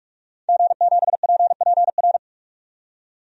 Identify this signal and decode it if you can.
Morse code, heard as a keyed tone, about 46 wpm, 700 Hz. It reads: G7PPR
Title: DXCC-100 Call Signs - 5 Characters - 45wpm